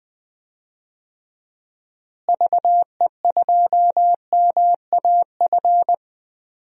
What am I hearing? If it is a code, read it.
Morse code, heard as a keyed tone, about 20 wpm, 700 Hz. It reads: VE2MAF
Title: DXCC-100 Call Signs - 6 Characters - Repeated 3x - 20wpm